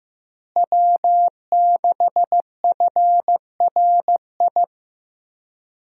Morse code, heard as a keyed tone, about 16 wpm, 700 Hz. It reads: W6FRI
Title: USA Call Signs - 5 Characters - Repeated 3x - 15wpm